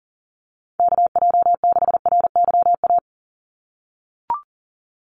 Morse code, heard as a keyed tone, about 40 wpm, 700 Hz. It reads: KJ6RYA
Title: USA Call Signs 40wpm